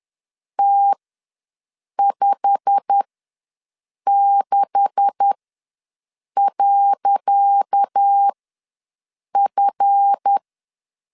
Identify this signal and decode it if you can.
Morse code, heard as a keyed tone, about 10 wpm, spaced as for 6 wpm, 800 Hz. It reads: T56.F
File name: LEZIONI / 10 SIGNS / 10C DOT ws 10wpm 4213